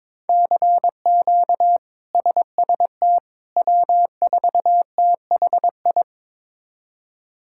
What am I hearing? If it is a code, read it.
Morse code, heard as a keyed tone, about 22 wpm, 700 Hz. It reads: CQ SST W4THI
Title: K1USN SST QSO Practice - 22wpm